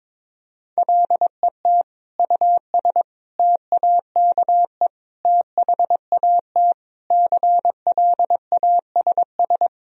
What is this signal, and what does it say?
Morse code, heard as a keyed tone, about 22 wpm, 700 Hz. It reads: LET US TAKE THAT CLASS
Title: Sentences from Top 1000 Words - Review Part 2 - 22wpm